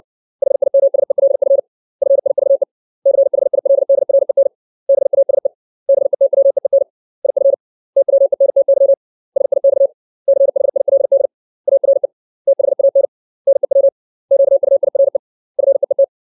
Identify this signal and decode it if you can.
Morse code, heard as a keyed tone, about 60 wpm, 550 Hz. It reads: HIMSELF FIVE CHILDREN BASE BETWEEN IF TWENTY SIX CHILD ARE THAN NEW QUIRE FEET
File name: 632 words / 632 Words 60WPM